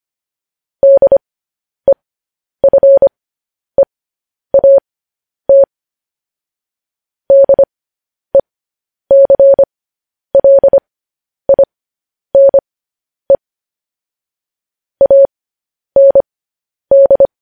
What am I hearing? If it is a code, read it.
Morse code, heard as a keyed tone, about 25 wpm, spaced as for 10 wpm, 550 Hz. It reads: DEFEAT DECLINE AND
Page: News Headlines in Morse Code at 10 WPM